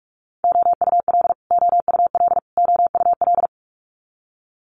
Morse code, heard as a keyed tone, about 46 wpm, 690 Hz. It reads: OUL OUL OUL
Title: Multi-letter Phonemes Repeated 3x 45wpm